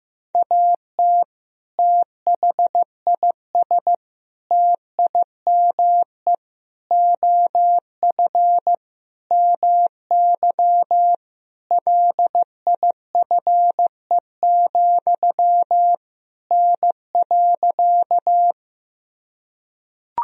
Morse code, heard as a keyed tone, about 15 wpm, 700 Hz. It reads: AT THIS TIME OF MY LIFE, N.
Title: Sentences from Top 300 Words and Abbreviations 15wpm